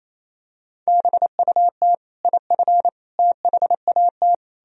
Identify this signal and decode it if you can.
Morse code, heard as a keyed tone, about 28 wpm, 700 Hz. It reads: BUT IF THAT
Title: Sentences from Top 200 Words and Abbreviations 28wpm